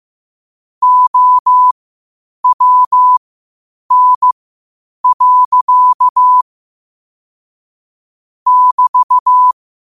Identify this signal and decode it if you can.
Morse code, heard as a keyed tone, about 15 wpm, spaced as for 8 wpm, 1000 Hz. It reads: OWN. <BT>
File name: qotdfiles / 8wpm